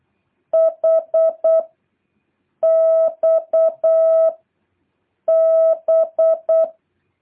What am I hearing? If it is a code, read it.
Morse code, heard as a keyed tone, about 8 wpm, 650 Hz. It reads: H X B
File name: jh8cbh / 2023cw